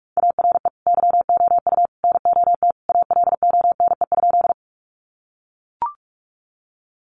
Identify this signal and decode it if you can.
Morse code, heard as a keyed tone, about 44 wpm, 700 Hz. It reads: ARE YOU NOT ALONE?
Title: Sentences from Top 1000 Words 45wpm